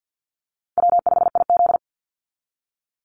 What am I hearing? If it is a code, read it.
Morse code, heard as a keyed tone, about 50 wpm, 700 Hz. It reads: W5IZ
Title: Easy Call Signs - 50wpm